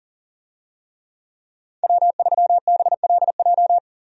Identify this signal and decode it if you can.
Morse code, heard as a keyed tone, about 37 wpm, 700 Hz. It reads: W3BLJ
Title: USA Call Signs - 5 Characters - 40wpm